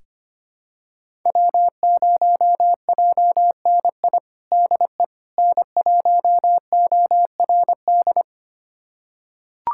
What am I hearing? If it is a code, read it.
Morse code, heard as a keyed tone, about 25 wpm, 700 Hz. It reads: W0JNI DE N1ORD
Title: USA Callsign DE Callsign - 5 Characters - 25wpm